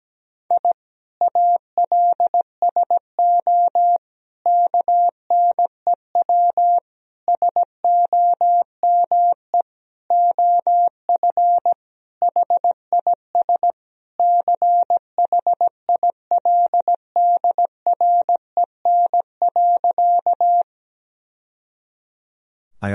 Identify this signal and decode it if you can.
Morse code, heard as a keyed tone, about 17 wpm, 700 Hz. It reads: I ALSO KNEW SOME OF HIS CHILDREN.
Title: Sentences from Top 1000 Words - Review Part 1 - 17wpm